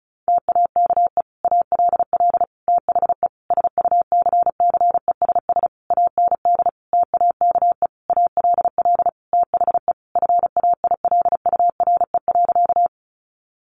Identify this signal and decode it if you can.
Morse code, heard as a keyed tone, about 34 wpm, 700 Hz. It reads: TAKE ALL THE SUCCESS AND TAKE ALL THE FAILURE.
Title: Sentences from Top 900 Words 35wpm